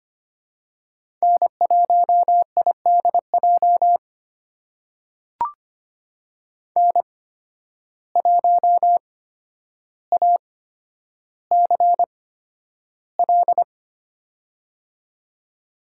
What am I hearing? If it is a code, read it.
Morse code, heard as a keyed tone, about 25 wpm, spaced as for 11 wpm, 700 Hz. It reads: N1IDJ N1ACL
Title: USA Call Signs - 8x Character Spacing - 25wpm